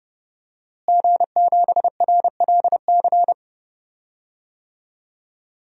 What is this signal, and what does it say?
Morse code, heard as a keyed tone, about 32 wpm, 700 Hz. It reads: G7RLC
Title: DXCC-100 Call Signs - 5 Characters - Repeated 2x - 30wpm